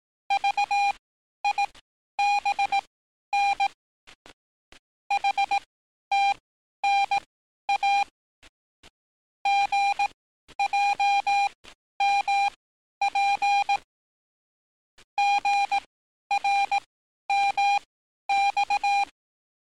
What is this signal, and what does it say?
Morse code, heard as a keyed tone, about 18 wpm, spaced as for 11 wpm, 800 Hz. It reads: VIBN HTNA GJMP GRMX
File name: training / NAVPERS 11049to11060 / NAVPERS 11051 RA RB